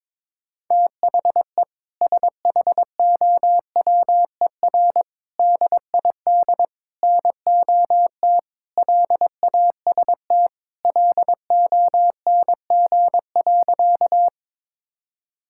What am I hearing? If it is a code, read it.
Morse code, heard as a keyed tone, about 22 wpm, 700 Hz. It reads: THE SHOWER DID NOT LAST LONG.
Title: Sentences from Top 1600 Words 22wpm